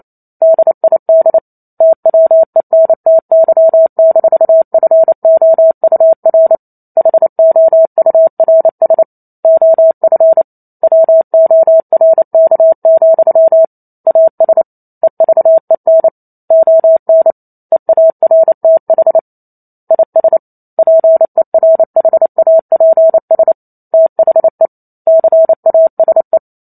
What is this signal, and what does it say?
Morse code, heard as a keyed tone, about 28 wpm, 650 Hz. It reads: DID TWENTY-FOUR HOURS OF WORK, AS EVEN ON EARTH IS PERHAPS THE CASE